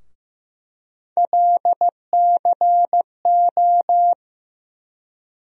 Morse code, heard as a keyed tone, about 15 wpm, 700 Hz. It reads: LCO
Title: USA Call Sign Postfixes - 3 Letters 15wpm